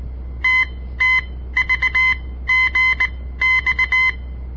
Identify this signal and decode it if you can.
Morse code, heard as a keyed tone, about 18 wpm, spaced as for 14 wpm, 2050 Hz. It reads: TTVGX
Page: On the air Morse code practice from the Southern Montana Amatuer Radio Association; all code, no read back